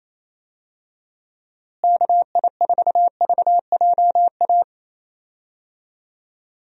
Morse code, heard as a keyed tone, about 28 wpm, 700 Hz. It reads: KI4VJA
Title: USA Call Signs - 6 Characters - Repeated 2x - 28wpm